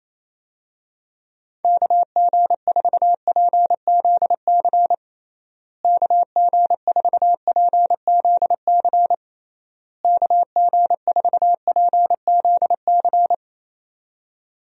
Morse code, heard as a keyed tone, about 28 wpm, 700 Hz. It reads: KG4PZC KG4PZC KG4PZC
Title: USA Call Signs - 6 Characters - Repeated 3x - 28wpm